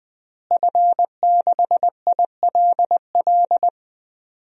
Morse code, heard as a keyed tone, about 20 wpm, 700 Hz. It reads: F6ILL
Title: DXCC-100 Call Signs - 5 Characters - Repeated 2x - 20wpm